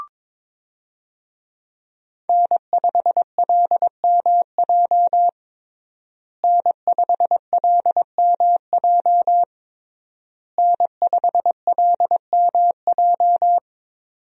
Morse code, heard as a keyed tone, about 22 wpm, 700 Hz. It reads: N5LMJ N5LMJ N5LMJ
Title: USA Call Signs - 5 Characters - Repeated 3x - 22wpm